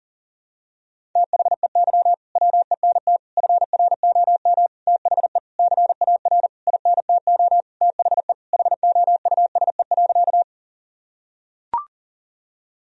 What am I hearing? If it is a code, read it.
Morse code, heard as a keyed tone, about 40 wpm, 700 Hz. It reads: THEY WENT FROM THE CAR INTO THE HOUSE.
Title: Sentences from Top 1000 Words - Review Part 1 - 40wpm